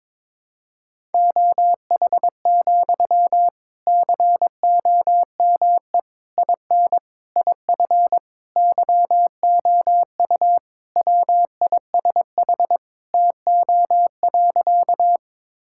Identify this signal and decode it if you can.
Morse code, heard as a keyed tone, about 22 wpm, 700 Hz. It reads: OH, COME IN IF YOU WISH TO.
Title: Sentences from Top 1600 Words 22wpm